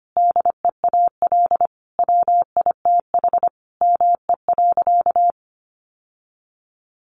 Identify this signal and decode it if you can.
Morse code, heard as a keyed tone, about 25 wpm, 700 Hz. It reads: DEAL WITH ME.
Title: Sentences from Top 800 Words 25wpm